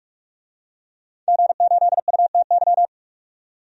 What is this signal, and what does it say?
Morse code, heard as a keyed tone, about 47 wpm, 700 Hz. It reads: G8UTY
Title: DXCC-100 Call Signs - 5 Characters - 45wpm